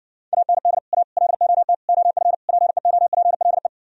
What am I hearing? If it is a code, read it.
Morse code, heard as a keyed tone, about 60 wpm, 700 Hz. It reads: AND A LOT OF PEOPLE